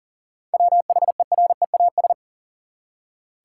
Morse code, heard as a keyed tone, about 39 wpm, 700 Hz. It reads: WHEREAS
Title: Seven Letter Words - 40wpm